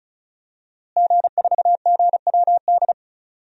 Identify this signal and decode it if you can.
Morse code, heard as a keyed tone, about 36 wpm, 700 Hz. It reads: G4GWD
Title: DXCC-100 Call Signs - 5 Characters - Repeated 2x - 35wpm